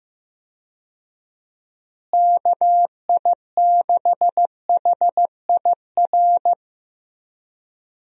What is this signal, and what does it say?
Morse code, heard as a keyed tone, about 15 wpm, 700 Hz. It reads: KI6HIR
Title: USA Call Signs - 6 Characters - Repeated 3x - 15wpm